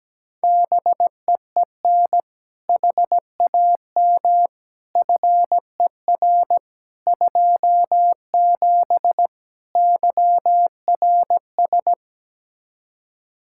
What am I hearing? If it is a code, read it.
Morse code, heard as a keyed tone, about 17 wpm, 700 Hz. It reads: BEEN HAM FER 27 YRS
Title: K1USN SST QSO Practice 17wpm